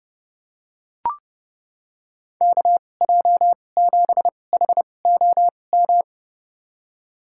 Morse code, heard as a keyed tone, about 30 wpm, spaced as for 22 wpm, 700 Hz. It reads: KJ7HOM
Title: USA Call Signs - 6 Characters - 2x Character Spacing - 30wpm